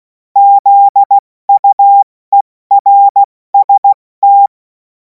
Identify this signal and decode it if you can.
Morse code, heard as a keyed tone, about 16 wpm, 800 Hz. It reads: ZUERST
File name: Morsekurs / Witze / Witz 24. Februar 2026 70 F80